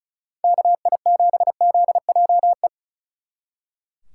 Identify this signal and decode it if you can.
Morse code, heard as a keyed tone, about 44 wpm, 700 Hz. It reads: KI7ZJE